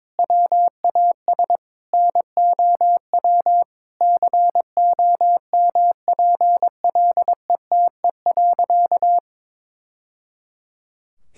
Morse code, heard as a keyed tone, about 22 wpm, 700 Hz. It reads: WAS NOW COMPLETE.
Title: Sentences from Top 1000 Words - Review Part 6 - 22wpm